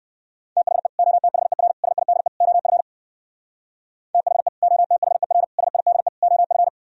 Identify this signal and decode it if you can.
Morse code, heard as a keyed tone, about 59 wpm, 700 Hz. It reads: THE OTHER SIDE OF THE OTHER SIDE OF